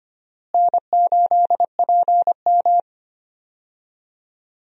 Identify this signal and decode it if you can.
Morse code, heard as a keyed tone, about 25 wpm, 700 Hz. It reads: N8PM
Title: USA Call Signs - 4 Characters - Repeated 2x - 25wpm